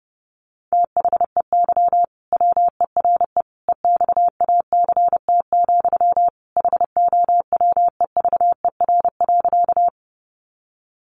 Morse code, heard as a keyed tone, about 30 wpm, 700 Hz. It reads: THEY WERE EXACT, HOWEVER.
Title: Sentences from Top 1300 Words 30wpm